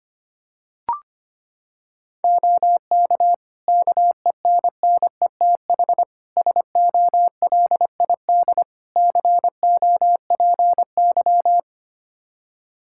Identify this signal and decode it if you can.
Morse code, heard as a keyed tone, about 25 wpm, 700 Hz. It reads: OK KENNETH SOLID COPY
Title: Practice QSO - Exchange 3 - More Info - 25wpm